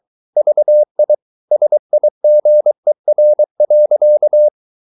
Morse code, heard as a keyed tone, about 23 wpm, 600 Hz. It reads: VI SIGER.
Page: Ferie 23 wpm